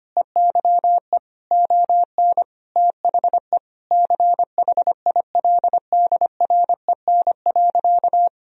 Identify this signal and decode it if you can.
Morse code, heard as a keyed tone, about 25 wpm, 700 Hz. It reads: EYE ON THE CHILDREN.
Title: Sentences from Top 300 Words and Abbreviations 25wpm